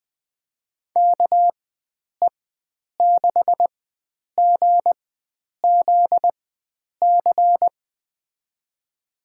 Morse code, heard as a keyed tone, about 20 wpm, spaced as for 9 wpm, 700 Hz. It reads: KE6GZC
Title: USA Call Signs - 6 Characters - 4x Character Spacing - 20wpm